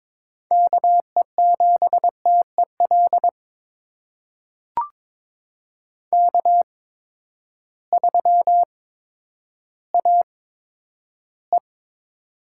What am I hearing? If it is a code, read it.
Morse code, heard as a keyed tone, about 22 wpm, spaced as for 11 wpm, 700 Hz. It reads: KE7TEL K3AE
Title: USA Call Signs - 8x Character Spacing - 22wpm